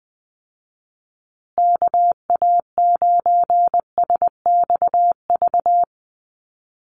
Morse code, heard as a keyed tone, about 20 wpm, 700 Hz. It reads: KA9SXV
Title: Call Signs Repeated 3x - 20wpm